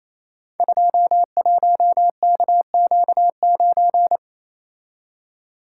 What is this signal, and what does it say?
Morse code, heard as a keyed tone, about 29 wpm, 700 Hz. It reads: 21KQ9
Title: Code Groups - 5 Letter:Numbers - 28wpm